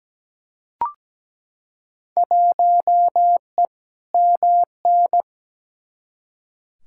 Morse code, heard as a keyed tone, about 17 wpm, 700 Hz. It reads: 1E MN
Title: Field Day Exchanges - 17wpm